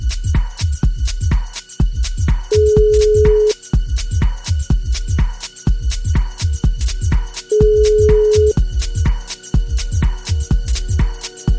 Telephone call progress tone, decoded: ringback tone

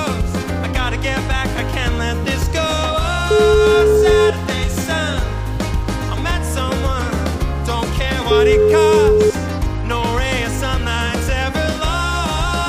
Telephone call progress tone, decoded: ringback tone

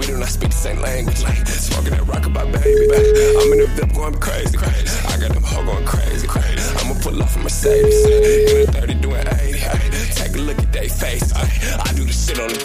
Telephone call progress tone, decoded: ringback tone

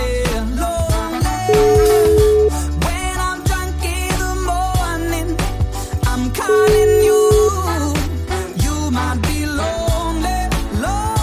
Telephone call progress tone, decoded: ringback tone

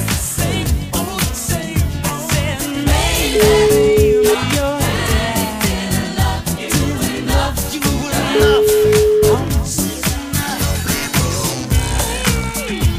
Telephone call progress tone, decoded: ringback tone

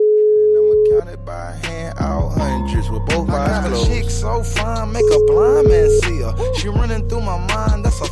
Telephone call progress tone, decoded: ringback tone

